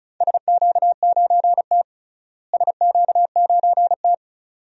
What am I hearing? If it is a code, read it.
Morse code, heard as a keyed tone, about 34 wpm, 700 Hz. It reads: SQ9T SQ9T
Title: DXCC-100 Call Signs - 4 Characters - Repeated 2x - 35wpm